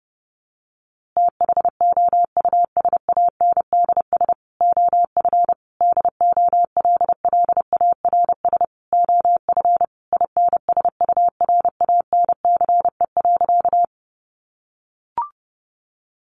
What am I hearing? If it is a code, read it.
Morse code, heard as a keyed tone, about 30 wpm, 700 Hz. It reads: THOUSANDS OF DOLLARS OF INSURANCE.
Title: A Sound Of Thunder 30wpm